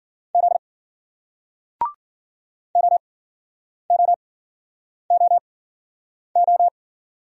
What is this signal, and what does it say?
Morse code, heard as a keyed tone, about 53 wpm, spaced as for 9 wpm, 700 Hz. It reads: Z OOOO